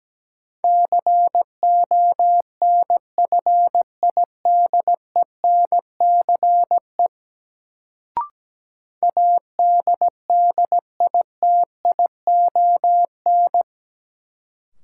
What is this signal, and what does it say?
Morse code, heard as a keyed tone, about 17 wpm, 700 Hz. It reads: CONFIDENCE ADDITION
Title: Sentences from Top 1500 Words - New Word Drill Alt - 17wpm